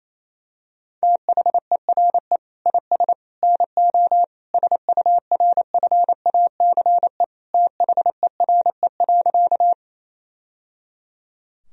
Morse code, heard as a keyed tone, about 28 wpm, 700 Hz. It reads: THERE IS NO SURFACE THERE.
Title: Sentences from Top 1000 Words - Review Part 3 - 28wpm